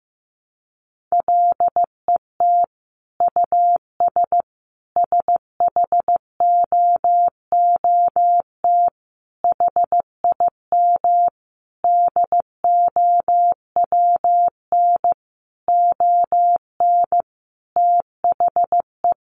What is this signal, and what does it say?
Morse code, heard as a keyed tone, about 15 wpm, 700 Hz. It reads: LET US SHOOT HIM DOWN ON THE
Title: Sentences from Top 1200 Words 15wpm